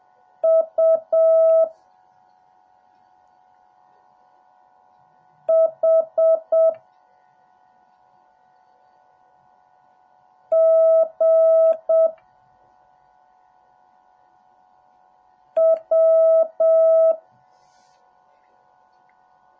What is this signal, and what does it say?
Morse code, heard as a keyed tone, about 7 wpm, 650 Hz. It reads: U H G W
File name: jh8cbh / 2021cw / cw08a